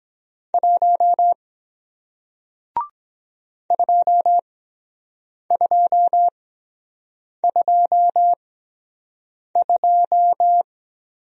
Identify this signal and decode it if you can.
Morse code, heard as a keyed tone, about 23 wpm, spaced as for 7 wpm, 700 Hz. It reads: 1 2222